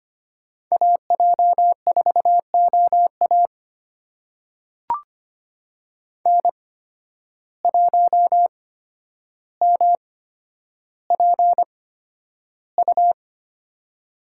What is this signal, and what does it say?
Morse code, heard as a keyed tone, about 25 wpm, spaced as for 11 wpm, 700 Hz. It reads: AJ4OA N1MPU
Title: USA Call Signs - 5 Characters - 8x Character Spacing - 25wpm